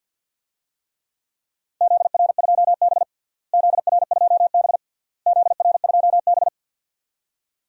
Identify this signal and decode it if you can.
Morse code, heard as a keyed tone, about 52 wpm, 700 Hz. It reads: ZR2B ZR2B ZR2B
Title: DXCC-100 Call Signs - 4 Characters - Repeated 3x - 50wpm